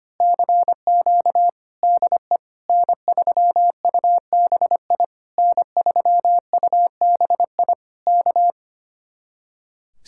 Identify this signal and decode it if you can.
Morse code, heard as a keyed tone, about 24 wpm, 700 Hz. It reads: CQ DE N3UBI N3UBI K
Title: Speed-Racing Common QSO Phrases - 25wpm